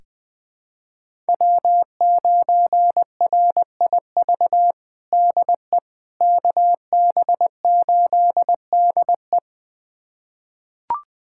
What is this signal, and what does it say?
Morse code, heard as a keyed tone, about 20 wpm, 700 Hz. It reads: W9RIV DE KB8DE
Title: USA Callsign DE Callsign - 5 Characters - 20wpm